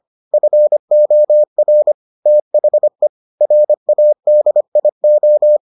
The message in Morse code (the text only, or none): FOR THE RADIO